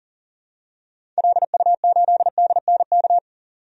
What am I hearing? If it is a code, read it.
Morse code, heard as a keyed tone, about 40 wpm, 700 Hz. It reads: LU8DNK
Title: DXCC-100 Call Signs - 6 Characters - 40wpm